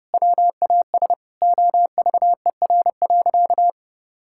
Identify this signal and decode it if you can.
Morse code, heard as a keyed tone, about 30 wpm, 700 Hz. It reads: WAS OVER.
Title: Sentences from Top 1000 Words - Review Part 6 - 30wpm